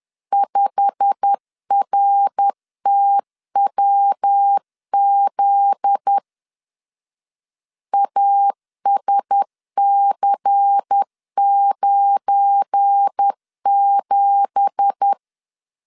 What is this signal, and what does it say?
Morse code, heard as a keyed tone, about 10 wpm, 800 Hz. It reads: 5RTWZ ASC97
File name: LEZIONI / 9 2867 10wpm 2733